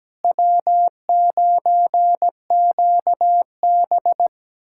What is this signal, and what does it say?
Morse code, heard as a keyed tone, about 17 wpm, 700 Hz. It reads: W9QB
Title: USA Callsign DE Callsign - 4 Characters - 17wpm